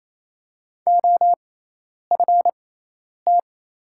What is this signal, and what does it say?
Morse code, heard as a keyed tone, about 28 wpm, spaced as for 10 wpm, 700 Hz. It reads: OFT
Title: 6x Character Spacing - Top 300 Words - 28wpm